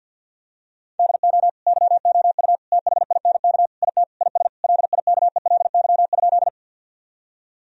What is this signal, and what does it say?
Morse code, heard as a keyed tone, about 46 wpm, 700 Hz. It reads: DO YOU THINK IT IS LIKELY?
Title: Sentences from Top 1000 Words - Review Part 4 - 50wpm